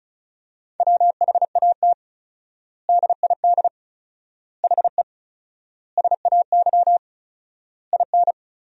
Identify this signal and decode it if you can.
Morse code, heard as a keyed tone, about 36 wpm, 700 Hz. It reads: WHAT DID HE SAY IN
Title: 4x Word-Spacing - Sentences from Top 300 Words - 35wpm